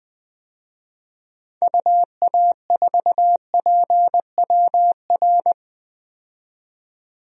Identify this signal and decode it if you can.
Morse code, heard as a keyed tone, about 21 wpm, 700 Hz. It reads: UA4PWR
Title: DXCC-100 Call Signs - 6 Characters - 20wpm